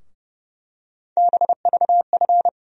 Morse code, heard as a keyed tone, about 31 wpm, 700 Hz. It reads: BVF
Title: USA Call Sign Postfixes - 3 Letters 30wpm